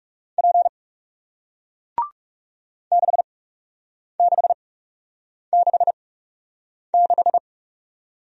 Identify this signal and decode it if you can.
Morse code, heard as a keyed tone, about 38 wpm, spaced as for 8 wpm, 700 Hz. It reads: P 6666